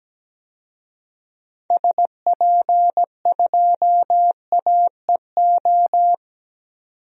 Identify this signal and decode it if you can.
Morse code, heard as a keyed tone, about 17 wpm, 700 Hz. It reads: SP2AEO
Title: DXCC-100 Call Signs - 6 Characters - 17wpm